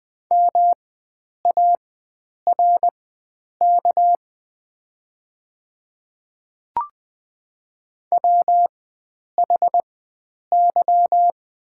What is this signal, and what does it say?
Morse code, heard as a keyed tone, about 20 wpm, spaced as for 9 wpm, 690 Hz. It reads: MARK WHY
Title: 4x Character Spacing - Top 300 Words - 20wpm